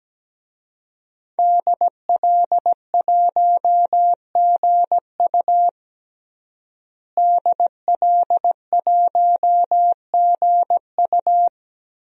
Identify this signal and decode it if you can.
Morse code, heard as a keyed tone, about 17 wpm, 700 Hz. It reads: DL1GU DL1GU
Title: DXCC-100 Call Signs - 5 Characters - Repeated 2x - 17wpm